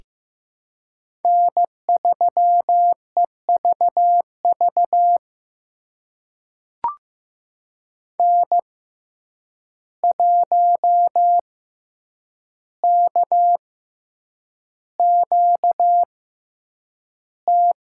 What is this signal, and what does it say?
Morse code, heard as a keyed tone, about 15 wpm, spaced as for 8 wpm, 700 Hz. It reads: N3EVV N1KQT
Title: USA Call Signs - 5 Characters - 6x Character Spacing - 15wpm